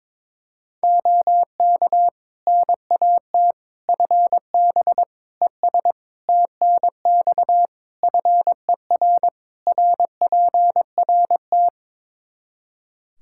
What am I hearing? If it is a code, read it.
Morse code, heard as a keyed tone, about 22 wpm, 700 Hz. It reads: OK NAT FB ES TNX FER RPRT
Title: Practice QSO - Exchange 2 - Info - 22wpm